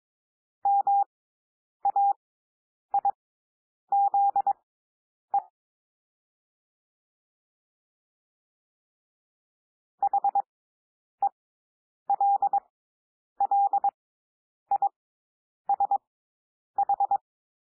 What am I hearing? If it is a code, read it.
Morse code, heard as a keyed tone, about 22 wpm, spaced as for 9 wpm, 800 Hz. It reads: MAIZE HELLISH
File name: SOUND WORDS 22 wpm